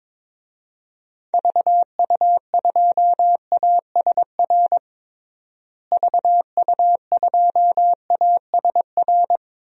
Morse code, heard as a keyed tone, about 22 wpm, 700 Hz. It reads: VU2ASR VU2ASR
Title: DXCC-100 Call Signs - 6 Characters - Repeated 2x - 22wpm